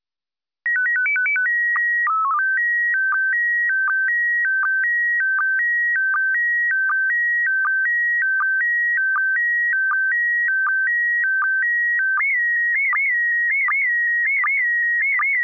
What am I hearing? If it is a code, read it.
SSTV (slow-scan television), PD 180.